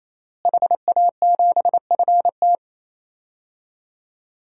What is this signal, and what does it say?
Morse code, heard as a keyed tone, about 28 wpm, 700 Hz. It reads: HA7FT
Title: DXCC-100 Call Signs - 5 Characters - Repeated 3x - 28wpm